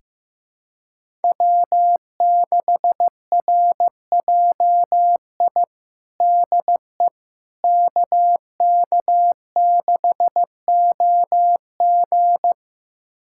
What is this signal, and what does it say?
Morse code, heard as a keyed tone, about 15 wpm, 700 Hz. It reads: W6RJI DE KK6OG
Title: USA Callsign DE Callsign - 5 Characters - 15wpm